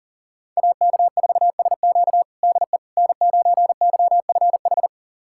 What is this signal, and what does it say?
Morse code, heard as a keyed tone, about 40 wpm, 700 Hz. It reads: AK4SQ DE N9YFH
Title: USA Callsign DE Callsign - 5 Characters - 40wpm